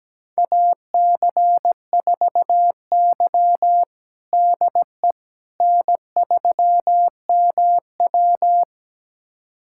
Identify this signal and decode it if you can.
Morse code, heard as a keyed tone, about 17 wpm, 700 Hz. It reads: AC4Y DE N3MW
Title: USA Callsign DE Callsign - 4 Characters - 17wpm